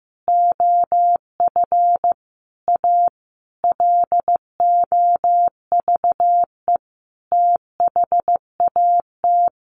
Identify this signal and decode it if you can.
Morse code, heard as a keyed tone, about 15 wpm, 700 Hz. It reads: OF A LOVE THAT
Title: Sentences from Top 1500 Words 15wpm